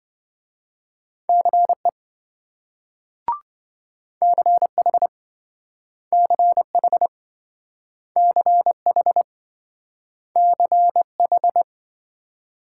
Morse code, heard as a keyed tone, about 27 wpm, 700 Hz. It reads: CE CH CH CH CH